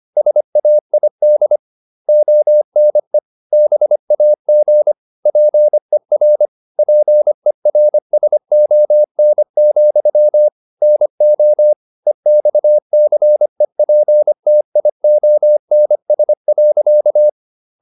Morse code, heard as a keyed tone, about 25 wpm, 600 Hz. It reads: SAID ONE BAG PER PERSON, NO EXCEPTIONS.